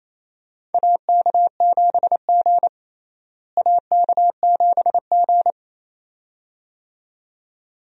Morse code, heard as a keyed tone, about 28 wpm, 700 Hz. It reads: AK7G AK7G
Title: USA Call Signs - 4 Characters - Repeated 2x - 28wpm